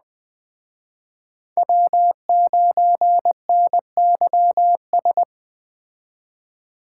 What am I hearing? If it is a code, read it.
Morse code, heard as a keyed tone, about 21 wpm, 700 Hz. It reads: W9NYS